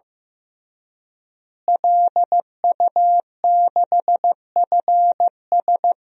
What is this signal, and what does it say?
Morse code, heard as a keyed tone, about 16 wpm, 700 Hz. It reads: LU6FS
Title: DXCC-100 Call Signs - 5 Characters - Repeated 2x - 15wpm